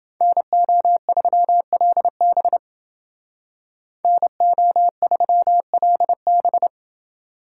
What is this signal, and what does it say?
Morse code, heard as a keyed tone, about 29 wpm, 700 Hz. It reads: NO3LB NO3LB